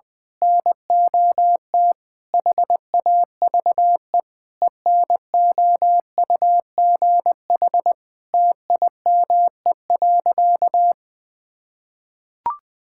Morse code, heard as a keyed tone, about 20 wpm, 700 Hz. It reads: NOT HAVE ENOUGH TIME.
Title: Sentences with 4-Word Phrases - 20wpm